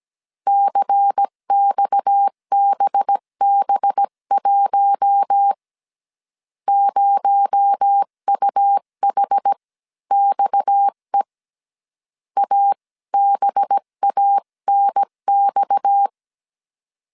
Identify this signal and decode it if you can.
Morse code, heard as a keyed tone, about 17 wpm, spaced as for 15 wpm, 800 Hz. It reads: CXBB1 0UHXE ABANX
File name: CW / DET / 1 6 1500C S C 16wpm 2733